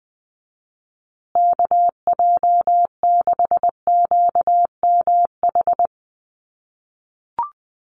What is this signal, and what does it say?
Morse code, heard as a keyed tone, about 20 wpm, 700 Hz. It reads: KJ6QMH
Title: USA Call Signs Repeated 2x 20wpm